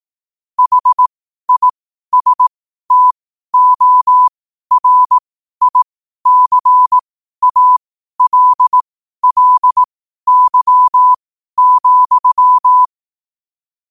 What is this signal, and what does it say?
Morse code, heard as a keyed tone, about 18 wpm, spaced as for 12 wpm, 1000 Hz. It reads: HISTORICALLY,
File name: qotdfiles / 12wpm